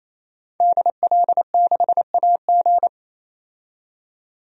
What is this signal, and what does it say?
Morse code, heard as a keyed tone, about 27 wpm, 700 Hz. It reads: DL6AG
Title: DXCC-100 Call Signs - 5 Characters - 28wpm